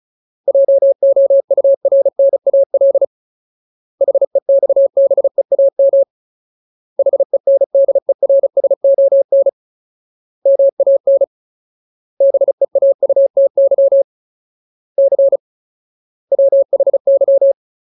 Morse code, heard as a keyed tone, about 34 wpm, 550 Hz. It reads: JOURNAL HEXBEAM HENDERSON MAN BEAUTY C WHY
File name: Ultimate Word List / 35WPM Ultimate